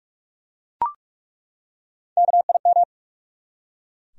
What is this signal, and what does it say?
Morse code, heard as a keyed tone, about 42 wpm, 700 Hz. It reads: KIM
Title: USA Common Names - 45wpm